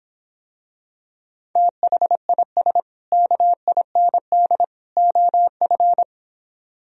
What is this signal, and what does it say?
Morse code, heard as a keyed tone, about 26 wpm, 700 Hz. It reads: THIS KIND OF